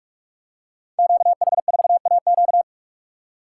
Morse code, heard as a keyed tone, about 41 wpm, 700 Hz. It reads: QH4AQ
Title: Code Groups - 5 Letter:Numbers - 45wpm